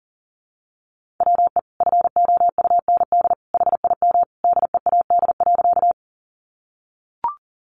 Morse code, heard as a keyed tone, about 40 wpm, 700 Hz. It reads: WE FOUND HIM DEAD.
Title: Sentences from Top 1500 Words 40wpm